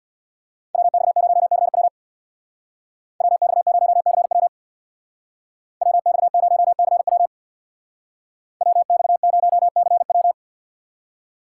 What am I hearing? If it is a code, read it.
Morse code, heard as a keyed tone, about 64 wpm, 700 Hz. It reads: WX0CW WX0CW WX0CW WX0CW